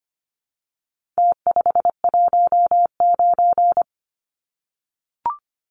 Morse code, heard as a keyed tone, about 25 wpm, 700 Hz. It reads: T519
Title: Numbers - Contest Style - 4-Digit 25wpm